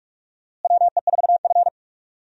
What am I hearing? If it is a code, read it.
Morse code, heard as a keyed tone, about 46 wpm, 700 Hz. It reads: WE4F
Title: USA Call Signs - 4 Characters - Repeated 2x - 45wpm